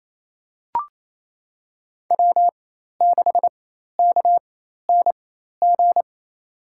Morse code, heard as a keyed tone, about 28 wpm, spaced as for 13 wpm, 700 Hz. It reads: W6KNG
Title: USA Call Signs - 5 Characters - 4x Character Spacing - 28wpm